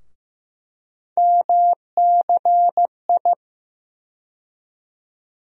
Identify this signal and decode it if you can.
Morse code, heard as a keyed tone, about 15 wpm, 700 Hz. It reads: MCI
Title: USA Call Sign Postfixes - 3 Letters 15wpm